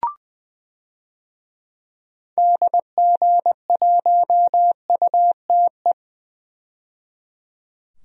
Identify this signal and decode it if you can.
Morse code, heard as a keyed tone, about 21 wpm, 700 Hz. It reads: DG1UTE
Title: DXCC-100 Call Signs - 6 Characters - 20wpm